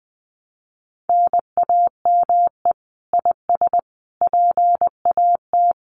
Morse code, heard as a keyed tone, about 20 wpm, 700 Hz. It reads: NAME IS PAT
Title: Common QSO Phrases 20wpm